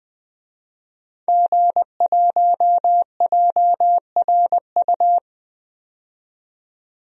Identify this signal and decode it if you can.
Morse code, heard as a keyed tone, about 20 wpm, 700 Hz. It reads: G1JRU
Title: DXCC-100 Call Signs - 5 Characters - 20wpm